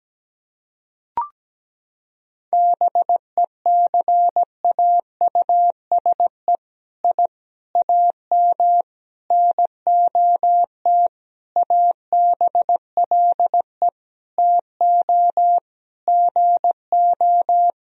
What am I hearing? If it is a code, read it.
Morse code, heard as a keyed tone, about 17 wpm, 700 Hz. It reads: BECAUSE I AM NOT ABLE TO GO
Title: Sentences from Top 1000 Words - Review Part 3 - 17wpm